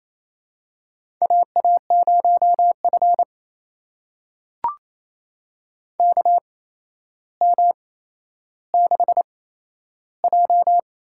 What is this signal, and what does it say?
Morse code, heard as a keyed tone, about 28 wpm, spaced as for 12 wpm, 700 Hz. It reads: AA0F KM6J
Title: USA Call Signs - 4 Characters - 8x Character Spacing - 28wpm